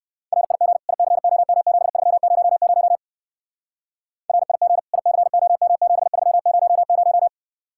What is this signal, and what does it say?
Morse code, heard as a keyed tone, about 56 wpm, 700 Hz. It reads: RIG ICOM7300 RIG ICOM7300